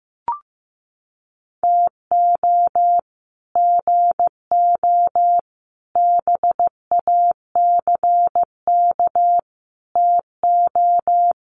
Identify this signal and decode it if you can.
Morse code, heard as a keyed tone, about 15 wpm, 700 Hz. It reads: TO GO BACK TO
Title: Sets of 5 Words 15wpm